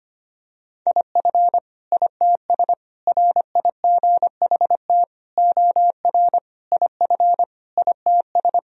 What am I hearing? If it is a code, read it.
Morse code, heard as a keyed tone, about 25 wpm, 700 Hz. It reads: IF ITS RIGHT OR IF ITS
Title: Sentences from Top 1000 Words - Review Part 3 - 25wpm